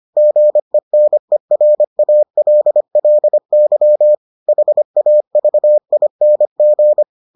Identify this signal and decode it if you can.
Morse code, heard as a keyed tone, about 25 wpm, 600 Hz. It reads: GENERALLY HAVING